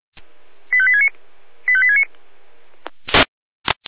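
Five-tone selective call (ZVEI), heard as quick runs of five tones, tones of about 80 ms. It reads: 85869 85869
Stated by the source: ZVEI five-tone call received on SDR.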